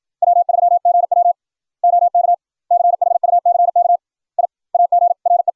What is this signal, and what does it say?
Morse code, heard as a keyed tone, about 56 wpm, 700 Hz. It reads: W3GW OK CHUCK I AGRE